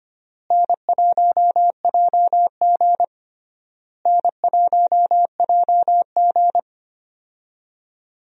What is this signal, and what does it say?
Morse code, heard as a keyed tone, about 25 wpm, 700 Hz. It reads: N1JG N1JG
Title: USA Call Signs - 4 Characters - Repeated 3x - 25wpm